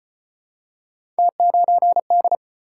Morse code, heard as a keyed tone, about 38 wpm, 700 Hz. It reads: T9D